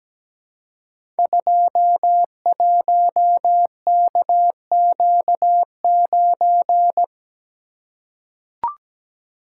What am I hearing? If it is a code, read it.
Morse code, heard as a keyed tone, about 17 wpm, 700 Hz. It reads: 21KQ9
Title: Code Groups - 5 Letter:Numbers - 17wpm